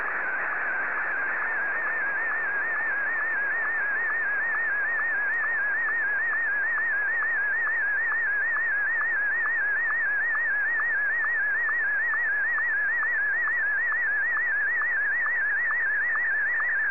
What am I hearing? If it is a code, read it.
SSTV (slow-scan television), Martin 1.